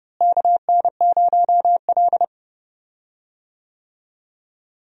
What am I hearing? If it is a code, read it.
Morse code, heard as a keyed tone, about 29 wpm, 700 Hz. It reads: KN0L
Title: USA Call Signs - 4 Characters - Repeated 2x - 30wpm